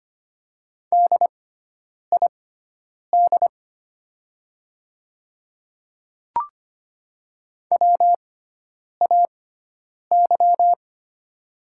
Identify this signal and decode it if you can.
Morse code, heard as a keyed tone, about 25 wpm, spaced as for 9 wpm, 700 Hz. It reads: DID WAY
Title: 6x Character Spacing - Top 200 Words - 25wpm